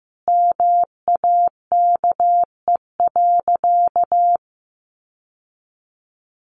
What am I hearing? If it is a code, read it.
Morse code, heard as a keyed tone, about 15 wpm, 700 Hz. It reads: MAKE.
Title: Sentences from Top 1400 Words 15wpm